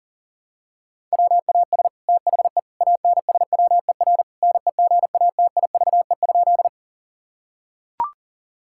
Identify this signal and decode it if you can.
Morse code, heard as a keyed tone, about 40 wpm, 690 Hz. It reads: WAS THE ANSWER NEGATIVE?
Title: Sentences from Top 1000 Words - Review Part 6 - 40wpm